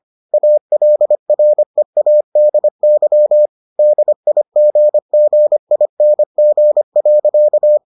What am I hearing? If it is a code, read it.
Morse code, heard as a keyed tone, about 25 wpm, 600 Hz. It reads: ALREADY DIGGING.